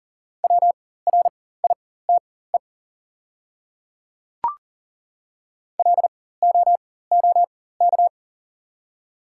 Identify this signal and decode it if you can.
Morse code, heard as a keyed tone, about 40 wpm, spaced as for 19 wpm, 700 Hz. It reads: WRITE LOOK
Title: 4x Character Spacing - Top 100 Words - 40wpm